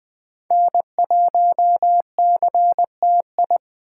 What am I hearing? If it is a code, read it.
Morse code, heard as a keyed tone, about 24 wpm, 700 Hz. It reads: N1CTI
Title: Speed Racing - USA Call Signs 20wpm